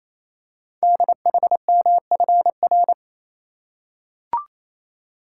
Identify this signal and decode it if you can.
Morse code, heard as a keyed tone, about 27 wpm, 700 Hz. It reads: DHMFR
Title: Code Groups - 5 Letters - 28wpm